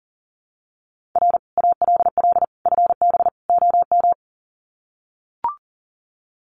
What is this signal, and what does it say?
Morse code, heard as a keyed tone, about 40 wpm, 700 Hz. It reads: R ALL FB OM
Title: Common QSO Phrases 40wpm